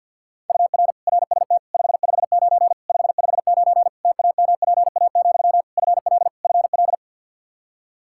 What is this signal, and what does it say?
Morse code, heard as a keyed tone, about 52 wpm, 700 Hz. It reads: UR RST 559 559 TAMPA, FL FL
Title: Practice QSO - Exchange 1 - Beginning - 50wpm